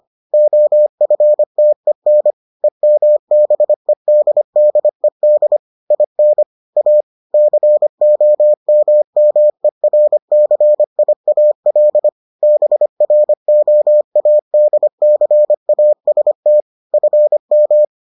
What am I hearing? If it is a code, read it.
Morse code, heard as a keyed tone, about 25 wpm, 600 Hz. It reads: OFTEN EMBEDDED IN A COMMERCIAL BROADCAST FM